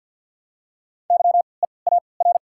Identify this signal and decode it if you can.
Morse code, heard as a keyed tone, about 52 wpm, spaced as for 29 wpm, 700 Hz. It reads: YEAR
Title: 3x Character Spacing - Top 300 Words - 50wpm